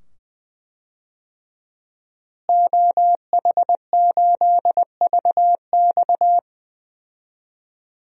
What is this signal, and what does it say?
Morse code, heard as a keyed tone, about 21 wpm, 700 Hz. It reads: OH8VX
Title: DXCC-100 Call Signs - 5 Characters - Repeated 2x - 20wpm